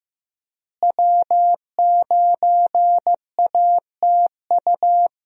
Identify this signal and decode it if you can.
Morse code, heard as a keyed tone, about 15 wpm, 700 Hz. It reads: W9ATU
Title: USA Call Signs - 5 Characters - Repeated 3x - 15wpm